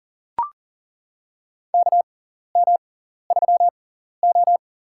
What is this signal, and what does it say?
Morse code, heard as a keyed tone, about 40 wpm, spaced as for 14 wpm, 700 Hz. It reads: KM3O
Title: USA Call Signs - 4 Characters - 6x Character Spacing - 40wpm